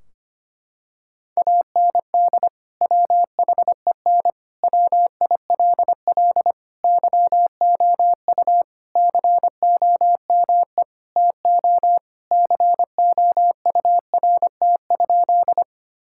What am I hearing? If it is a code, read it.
Morse code, heard as a keyed tone, about 25 wpm, 700 Hz. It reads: AND WHEN WILL YOU COME TO COURT?
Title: Sentences from Top 1600 Words 25wpm